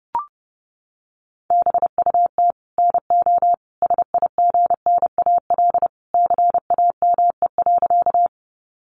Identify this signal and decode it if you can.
Morse code, heard as a keyed tone, about 28 wpm, 700 Hz. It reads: BUT NO SIGNAL CAME.
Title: Sentences from Top 1400 Words 30wpm